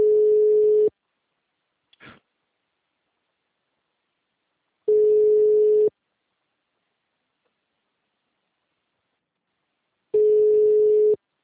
Ringback tone (Europe, 425 Hz).
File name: out- -2014- - - .14688